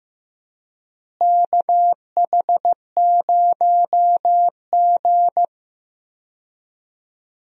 Morse code, heard as a keyed tone, about 16 wpm, 700 Hz. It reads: KH0G
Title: USA Call Signs - 4 Characters - Repeated 3x - 15wpm